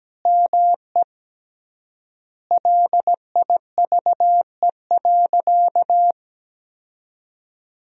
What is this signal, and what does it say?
Morse code, heard as a keyed tone, about 17 wpm, 700 Hz. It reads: ME LIVE.
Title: 3x Word-Spacing - Sentences from Top 300 Words - 17wpm